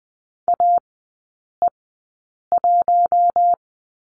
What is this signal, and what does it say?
Morse code, heard as a keyed tone, about 20 wpm, spaced as for 8 wpm, 700 Hz. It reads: AE1
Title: 060 Lesson 11e - Letter h - Three Characters 20wpm